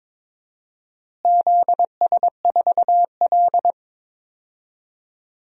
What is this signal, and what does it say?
Morse code, heard as a keyed tone, about 22 wpm, 700 Hz. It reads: ZS4L
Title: DXCC-100 Call Signs - 4 Characters - Repeated 2x - 22wpm